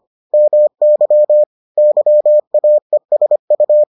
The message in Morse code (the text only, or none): MY YAESU